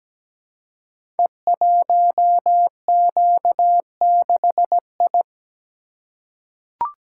Morse code, heard as a keyed tone, about 17 wpm, 700 Hz. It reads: E1Q6I
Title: Code Groups - 5 Letter:Numbers - 17wpm